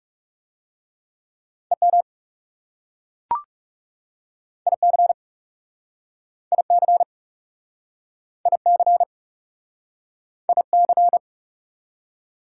Morse code, heard as a keyed tone, about 40 wpm, 690 Hz. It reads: EM IC IC IC IC